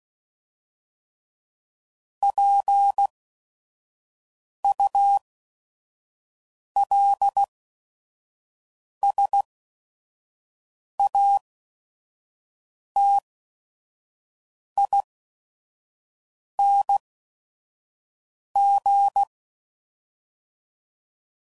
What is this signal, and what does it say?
Morse code, heard as a keyed tone, about 16 wpm, spaced as for 5 wpm, 800 Hz. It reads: PULSATING
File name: CW Code Groups @5WPM